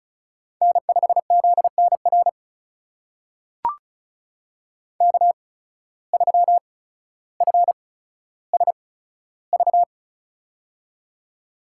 Morse code, heard as a keyed tone, about 36 wpm, spaced as for 15 wpm, 700 Hz. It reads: N5ZNR K3FSV
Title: USA Call Signs - 5 Characters - 8x Character Spacing - 35wpm